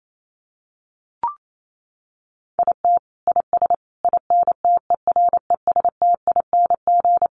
Morse code, heard as a keyed tone, about 28 wpm, 700 Hz. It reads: IT IS INTERESTING
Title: Sets of 5 Words Encore2 28wpm